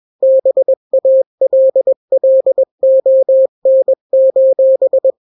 Morse code, heard as a keyed tone, about 21 wpm, 530 Hz. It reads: BALLON:
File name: daily / 20wpm